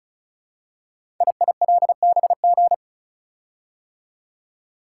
Morse code, heard as a keyed tone, about 36 wpm, 700 Hz. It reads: IILBG
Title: Code Groups - 5 Letters - 35wpm